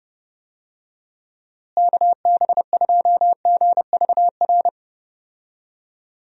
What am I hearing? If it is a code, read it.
Morse code, heard as a keyed tone, about 31 wpm, 700 Hz. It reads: KB2GVR
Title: USA Call Signs - 6 Characters - Repeated 3x - 30wpm